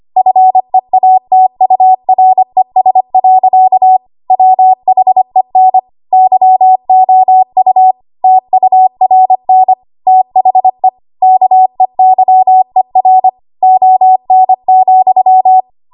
Morse code, heard as a keyed tone, about 25 wpm, 750 Hz. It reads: FEATURES. WHEN YOU TURN THE KEYER ON,
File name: Archive / 25 WPM / 25WPM